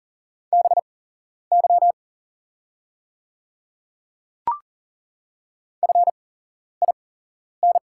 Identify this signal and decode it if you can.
Morse code, heard as a keyed tone, about 40 wpm, spaced as for 11 wpm, 700 Hz. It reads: BY FIN
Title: 8x Character Spacing - Top 100 Words - 40wpm